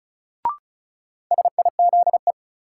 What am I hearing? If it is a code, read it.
Morse code, heard as a keyed tone, about 36 wpm, 700 Hz. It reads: SIZE
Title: Sets of 3 Words - New Word Drill - 35wpm